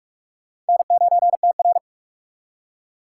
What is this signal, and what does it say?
Morse code, heard as a keyed tone, about 61 wpm, spaced as for 53 wpm, 700 Hz. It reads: N9TR
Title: Speed Racing - USA Call Signs 45wpm